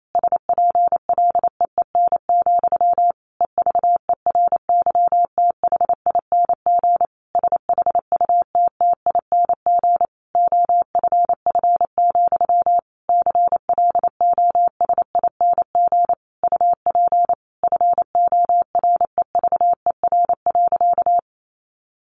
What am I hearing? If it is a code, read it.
Morse code, heard as a keyed tone, about 28 wpm, 700 Hz. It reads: SPLEEN, EVERYTHING SHUTTING OFF, CLOSING UP FOREVER.
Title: A Sound Of Thunder 28wpm